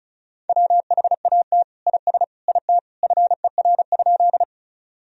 Morse code, heard as a keyed tone, about 36 wpm, 700 Hz. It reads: WHAT IS IT FER?
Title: Sentences from Top 100 Words and Abbreviations 35wpm